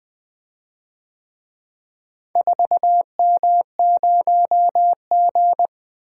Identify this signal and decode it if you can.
Morse code, heard as a keyed tone, about 24 wpm, 700 Hz. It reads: 4M0G
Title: Speed-Racing - Four Letter-Number Combinations - 20wpm